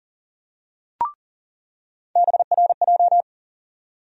Morse code, heard as a keyed tone, about 40 wpm, 700 Hz. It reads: BRJ
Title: USA Call Sign Postfixes - 3 Letters 40wpm